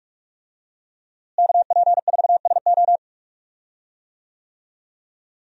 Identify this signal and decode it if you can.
Morse code, heard as a keyed tone, about 48 wpm, 700 Hz. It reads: KP4SO
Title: DXCC-100 Call Signs - 5 Characters - Repeated 2x - 45wpm